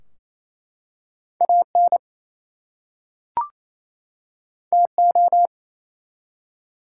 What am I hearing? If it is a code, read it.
Morse code, heard as a keyed tone, about 27 wpm, 700 Hz. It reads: AN TO
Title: Two Letter Words - 28wpm